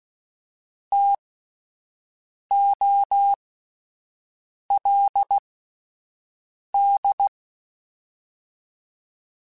Morse code, heard as a keyed tone, about 16 wpm, spaced as for 5 wpm, 800 Hz. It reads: TOLD
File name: Alphabetical / Alphabetical mp3s / Ts @5